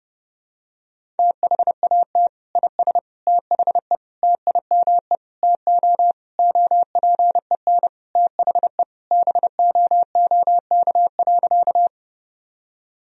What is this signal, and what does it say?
Morse code, heard as a keyed tone, about 32 wpm, 700 Hz. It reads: THAT IS THE TIME TO OPEN THE BOOK.
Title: Sentences from Top 1000 Words - Review Part 1 - 30wpm